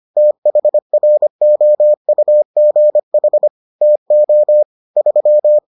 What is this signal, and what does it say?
Morse code, heard as a keyed tone, about 25 wpm, 600 Hz. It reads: THROUGH TO 3